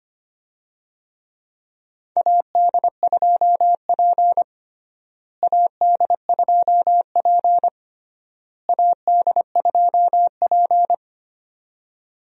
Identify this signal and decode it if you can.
Morse code, heard as a keyed tone, about 25 wpm, 700 Hz. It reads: AD2P AD2P AD2P
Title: USA Call Signs - 4 Characters - Repeated 3x - 25wpm